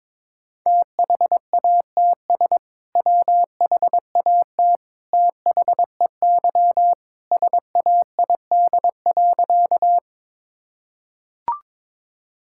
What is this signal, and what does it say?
Morse code, heard as a keyed tone, about 22 wpm, 700 Hz. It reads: THATS WHAT THEY SAID.
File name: practice / 22wpm / Sentences with 2-Word Phrases - 22wpm